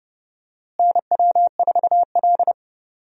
Morse code, heard as a keyed tone, about 29 wpm, 700 Hz. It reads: NW4L
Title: USA Call Signs - 4 Characters - Repeated 3x - 30wpm